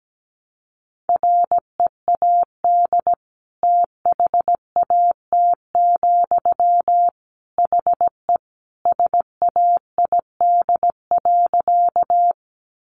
Morse code, heard as a keyed tone, about 17 wpm, 700 Hz. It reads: READ THAT, HE SAID.
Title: Sentences from Top 200 Words 17wpm